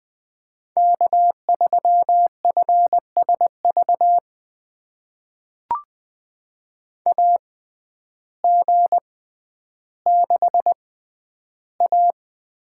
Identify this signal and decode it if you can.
Morse code, heard as a keyed tone, about 20 wpm, spaced as for 11 wpm, 700 Hz. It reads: K3FSV AG6A
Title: USA Call Signs - 6x Character Spacing - 20wpm